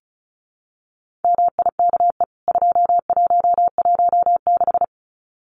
Morse code, heard as a keyed tone, about 34 wpm, 700 Hz. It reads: MIKE 2116
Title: CWT Exchange 35wpm